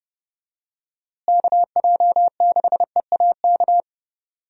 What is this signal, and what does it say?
Morse code, heard as a keyed tone, about 32 wpm, 700 Hz. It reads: KJ6EAK
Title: USA Call Signs - 6 Characters - Repeated 2x - 30wpm